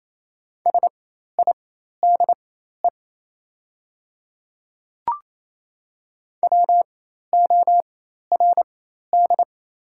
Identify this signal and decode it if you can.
Morse code, heard as a keyed tone, about 28 wpm, spaced as for 13 wpm, 700 Hz. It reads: SIDE WORD
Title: 4x Character Spacing - Top 100 Words - 28wpm